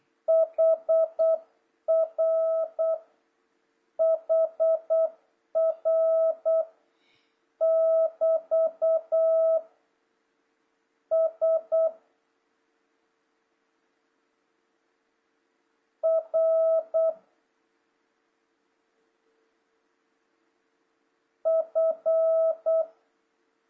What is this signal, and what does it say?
Morse code, heard as a keyed tone, about 8 wpm, 650 Hz. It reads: HR HR <BT> S R F
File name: jh8cbh / 2026cw